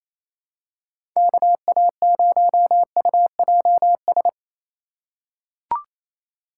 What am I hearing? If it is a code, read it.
Morse code, heard as a keyed tone, about 28 wpm, 700 Hz. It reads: KA0UJS
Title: USA Call Signs - 2x Character Spacing - 28wpm